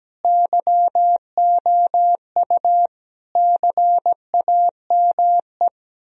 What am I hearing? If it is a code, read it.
Morse code, heard as a keyed tone, about 17 wpm, 700 Hz. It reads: YOU CAME
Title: Sentences from Top 1000 Words - Review Part 3 - 17wpm